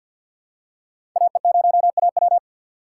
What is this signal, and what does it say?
Morse code, heard as a keyed tone, about 50 wpm, 700 Hz. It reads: AE0AW
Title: USA Call Signs - 5 Characters - 3x Character Spacing - 50wpm